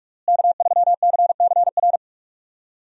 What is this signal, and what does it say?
Morse code, heard as a keyed tone, about 41 wpm, 700 Hz. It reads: K3CCR
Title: USA Call Signs - 5 Characters - Repeated 3x - 45wpm